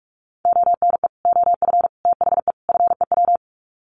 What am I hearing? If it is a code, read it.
Morse code, heard as a keyed tone, about 44 wpm, 700 Hz. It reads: ONE OF THE FEW
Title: Sets of 4 Words 45wpm